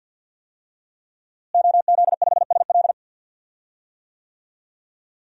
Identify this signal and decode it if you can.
Morse code, heard as a keyed tone, about 46 wpm, 700 Hz. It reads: OZ5SL
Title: DXCC-100 Call Signs - 5 Characters - Repeated 2x - 50wpm